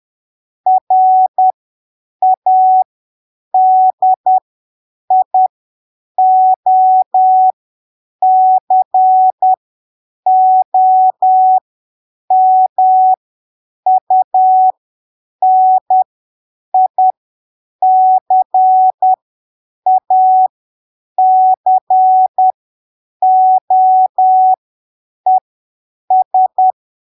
Morse code, heard as a keyed tone, about 10 wpm, spaced as for 7 wpm, 750 Hz. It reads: RADIOCOMUNICACOES
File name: arquivos / texto / texto 12 10ppm